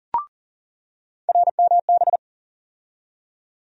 Morse code, heard as a keyed tone, about 40 wpm, 700 Hz. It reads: RMB
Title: USA Call Sign Postfixes - 3 Letters 40wpm